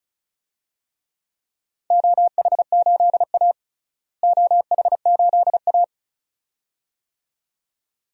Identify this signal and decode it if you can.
Morse code, heard as a keyed tone, about 34 wpm, 700 Hz. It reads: OH8A OH8A
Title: DXCC-100 Call Signs - 4 Characters - Repeated 2x - 35wpm